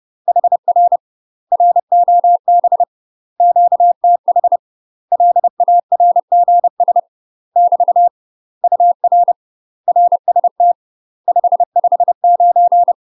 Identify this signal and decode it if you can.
Morse code, heard as a keyed tone, about 30 wpm, 700 Hz. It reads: HR ROB QTH LARGS <BT> UR RST 559
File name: UK MASTER QSOs TRAINING / 30WPM / UK MASTER QSOs 0039